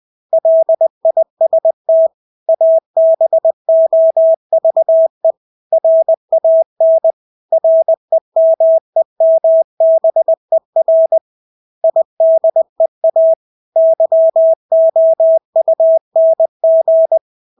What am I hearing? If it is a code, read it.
Morse code, heard as a keyed tone, about 20 wpm, 650 Hz. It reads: LIST ABOVE RAN REMEMBER IDEA YOUNG
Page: Engelske ord 20 wpm